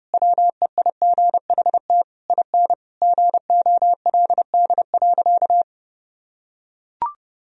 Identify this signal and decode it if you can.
Morse code, heard as a keyed tone, about 30 wpm, 700 Hz. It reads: WEIGHT IN GOLD.
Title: Sentences from Top 1000 Words - Review Part 3 - 30wpm